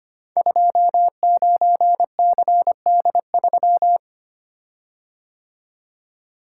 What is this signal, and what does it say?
Morse code, heard as a keyed tone, about 25 wpm, 700 Hz. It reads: 29CD3
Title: Code Groups - 5 Letter:Numbers - 25wpm